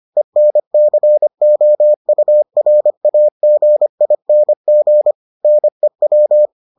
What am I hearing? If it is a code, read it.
Morse code, heard as a keyed tone, about 25 wpm, 600 Hz. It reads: ENCOURAGING NEW